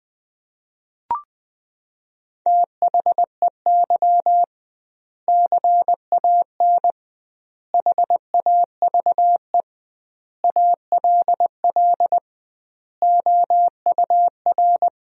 Morse code, heard as a keyed tone, about 20 wpm, 700 Hz. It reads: THEY CAN HAVE ALL OUR
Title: 2x Word-Spacing - Sentences from Top 400 Words - 20wpm